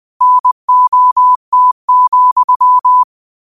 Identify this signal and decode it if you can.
Morse code, heard as a keyed tone, about 20 wpm, 1000 Hz. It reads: NOT,
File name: qotdfiles / 20wpm